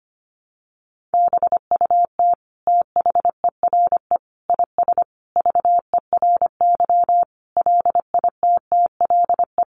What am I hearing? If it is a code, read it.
Morse code, heard as a keyed tone, about 25 wpm, 700 Hz. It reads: BUT THERE IS VERY LITTLE
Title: Sentences from Top 1000 Words 25wpm